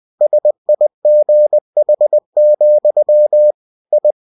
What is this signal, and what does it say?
Morse code, heard as a keyed tone, about 20 wpm, 600 Hz. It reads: SIGH, I
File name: cw-books / winnie-the-pooh / winnie-20wpm-page-0027